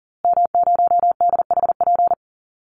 Morse code, heard as a keyed tone, about 40 wpm, 700 Hz. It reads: M0DHP
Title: Call Signs Repeated 2x - 40wpm